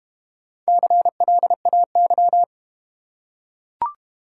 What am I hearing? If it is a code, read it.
Morse code, heard as a keyed tone, about 32 wpm, 690 Hz. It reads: CLAY